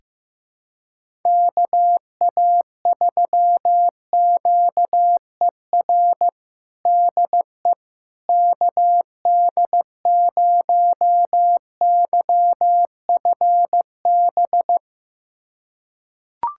Morse code, heard as a keyed tone, about 15 wpm, 700 Hz. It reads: KA3QER DE KD0YFB
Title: USA Callsign DE Callsign - 6 Characters - 15wpm